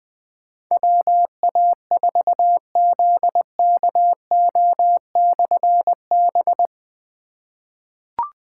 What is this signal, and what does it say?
Morse code, heard as a keyed tone, about 20 wpm, 700 Hz. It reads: WA4ZKO/B
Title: Difficult Call Signs 20wpm